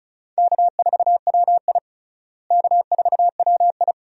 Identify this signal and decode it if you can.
Morse code, heard as a keyed tone, about 34 wpm, 700 Hz. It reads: K4WI K4WI
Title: USA Call Signs - 4 Characters - Repeated 2x - 35wpm